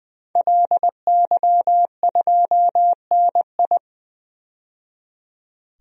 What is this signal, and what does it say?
Morse code, heard as a keyed tone, about 21 wpm, 700 Hz. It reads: LY2NI
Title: DXCC-100 Call Signs - 5 Characters - 20wpm